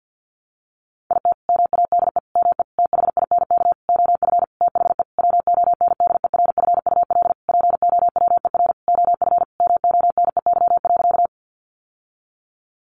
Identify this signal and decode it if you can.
Morse code, heard as a keyed tone, about 48 wpm, 700 Hz. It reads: IT MADE ME THINK OF THE WONDERFUL POWER OF MONEY.
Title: Sentences from Top 1100 Words 50wpm